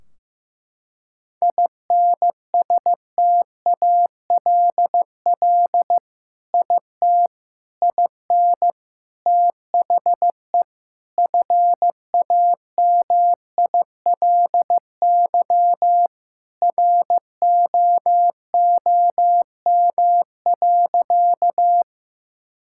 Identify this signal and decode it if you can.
Morse code, heard as a keyed tone, about 16 wpm, 700 Hz. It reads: INSTALL IT IN THE FAMILY ROOM.
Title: Sentences from Top 1600 Words 15wpm